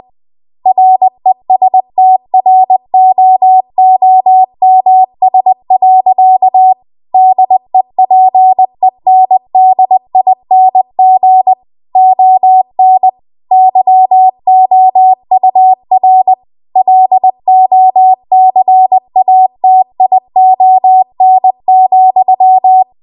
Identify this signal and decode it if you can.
Morse code, heard as a keyed tone, about 20 wpm, 750 Hz. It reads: RESTROOMS. DEPENDING ON YOUR LOCATION,